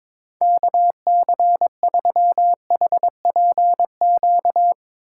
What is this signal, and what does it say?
Morse code, heard as a keyed tone, about 21 wpm, 700 Hz. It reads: KC3HPQ
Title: USA Call Signs - 6 Characters - 4x Character Spacing - 22wpm